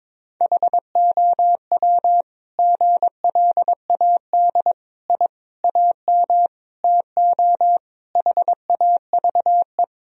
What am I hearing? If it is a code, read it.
Morse code, heard as a keyed tone, about 22 wpm, 700 Hz. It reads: HOW GLAD I AM TO HAVE
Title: Sentences from Top 1000 Words - Review Part 2 - 22wpm